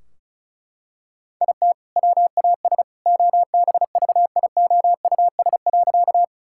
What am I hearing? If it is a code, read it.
Morse code, heard as a keyed tone, about 36 wpm, 700 Hz. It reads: IT WAS OBVIOUS.
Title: Sentences from Top 1600 Words 35wpm